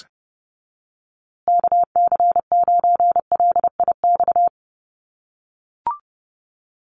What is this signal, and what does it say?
Morse code, heard as a keyed tone, about 30 wpm, 700 Hz. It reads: KC9LIX
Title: USA Call Signs Repeated 3x 30wpm